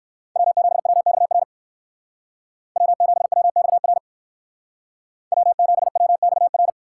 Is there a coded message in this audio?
Yes: Morse code from 0.4 to 6.7 s, heard as a keyed tone, about 59 wpm, 700 Hz. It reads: W7WXR W7WXR W7WXR